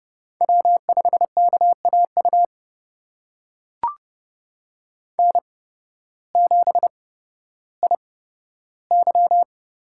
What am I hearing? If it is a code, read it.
Morse code, heard as a keyed tone, about 30 wpm, spaced as for 14 wpm, 700 Hz. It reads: W5KAU N7IY
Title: USA Call Signs - 8x Character Spacing - 30wpm